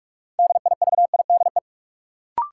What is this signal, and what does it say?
Morse code, heard as a keyed tone, about 46 wpm, 700 Hz. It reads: DIVIDE
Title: Sentences from Top 1600 Words - New Word Drill - 45wpm